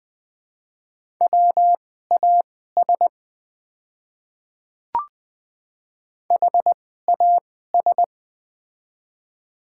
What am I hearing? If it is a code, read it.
Morse code, heard as a keyed tone, about 20 wpm, spaced as for 14 wpm, 700 Hz. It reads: WAS HAS
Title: 2x Character Spacing - Top 100 Words - 20wpm